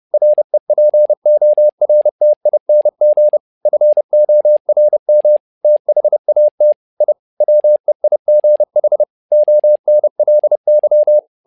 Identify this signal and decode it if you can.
Morse code, heard as a keyed tone, about 30 wpm, 600 Hz. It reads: REPORTING FORM THAT I WEIGH ONLY
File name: learning center / tower trouble / tower trouble 30wpm